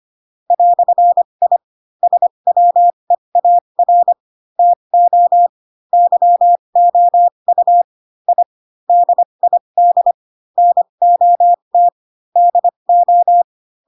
Morse code, heard as a keyed tone, about 25 wpm, spaced as for 22 wpm, 700 Hz. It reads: "I SWEAR TO YOU I DID NOT DO
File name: Pinocchio / Pinocchio ch2 18 25 ik1hge